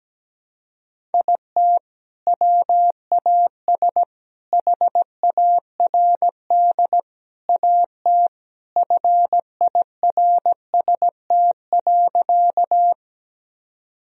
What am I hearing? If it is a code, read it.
Morse code, heard as a keyed tone, about 17 wpm, 700 Hz. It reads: IT WAS HARD AT FIRST.
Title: Sentences from Top 1000 Words - Review Part 1 - 17wpm